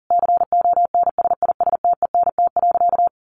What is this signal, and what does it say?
Morse code, heard as a keyed tone, about 40 wpm, 700 Hz. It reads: CONSISTENT.
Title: Sentences from Top 1500 Words 40wpm